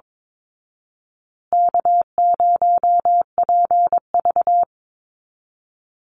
Morse code, heard as a keyed tone, about 22 wpm, 700 Hz. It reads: K0PV
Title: Call Signs Repeated 2x - 22wpm